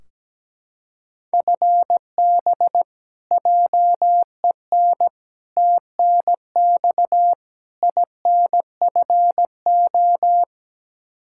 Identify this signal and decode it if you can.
Morse code, heard as a keyed tone, about 17 wpm, 700 Hz. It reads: FB JEN TNX INFO
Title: Practice QSO - Exchange 3 - More Info - 17wpm